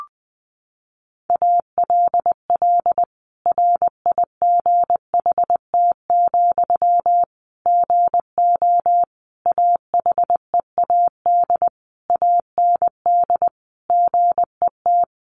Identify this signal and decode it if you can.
Morse code, heard as a keyed tone, about 20 wpm, 700 Hz. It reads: ALL RIGHT, GO AHEAD AND GET
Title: Sentences from Top 1100 Words 20wpm